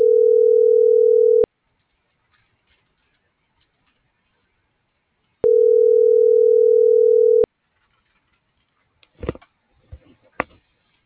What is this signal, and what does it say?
Ringback tone (North America), 440 + 480 Hz.